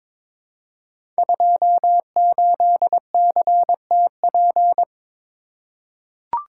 Morse code, heard as a keyed tone, about 22 wpm, 700 Hz. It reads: 28CTP
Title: Code Groups - 5 Letter:Numbers - 22wpm